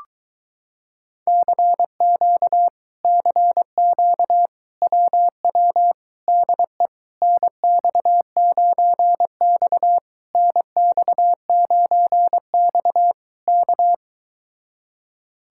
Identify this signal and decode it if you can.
Morse code, heard as a keyed tone, about 22 wpm, 700 Hz. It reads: CQ CQ WW DE NX9X NX9X K